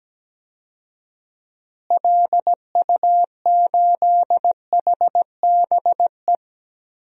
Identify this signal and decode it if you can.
Morse code, heard as a keyed tone, about 17 wpm, 700 Hz. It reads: LU8HBE
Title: DXCC-100 Call Signs - 6 Characters - Repeated 3x - 17wpm